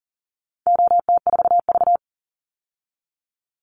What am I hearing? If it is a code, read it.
Morse code, heard as a keyed tone, about 40 wpm, 700 Hz. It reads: OT4V
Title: Call Signs Repeated 3x - 40wpm